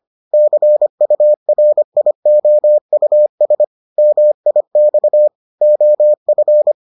Morse code, heard as a keyed tone, about 25 wpm, 600 Hz. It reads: CURIOUS MIX OF